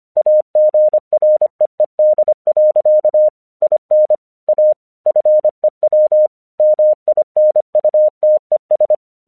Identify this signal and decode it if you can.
Morse code, heard as a keyed tone, about 26 wpm, 610 Hz. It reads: AGREED. IN A FEW MINUTES
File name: 1 English 25wpm / AQC20